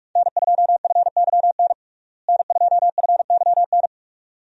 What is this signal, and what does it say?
Morse code, heard as a keyed tone, about 44 wpm, 700 Hz. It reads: N2FYN N2FYN
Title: USA Call Signs - 5 Characters - Repeated 2x - 45wpm